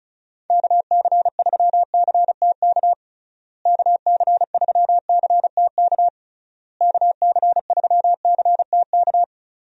Morse code, heard as a keyed tone, about 36 wpm, 700 Hz. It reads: KC3CTK KC3CTK KC3CTK
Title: USA Call Signs - 6 Characters - Repeated 3x - 35wpm